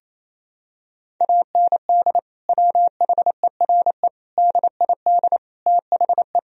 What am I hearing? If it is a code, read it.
Morse code, heard as a keyed tone, about 28 wpm, 700 Hz. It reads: AND WHERE DID THE
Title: Sentences from Top 1000 Words - Review Part 1 - 28wpm